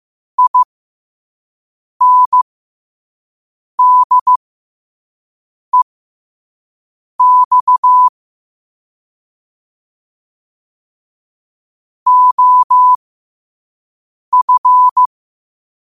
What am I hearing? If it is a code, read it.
Morse code, heard as a keyed tone, about 15 wpm, spaced as for 5 wpm, 1000 Hz. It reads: INDEX OF
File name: qotdfiles / 5wpm